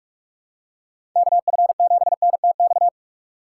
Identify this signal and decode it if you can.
Morse code, heard as a keyed tone, about 44 wpm, 700 Hz. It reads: KF7NTX
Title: USA Call Signs - 6 Characters - 6x Character Spacing - 45wpm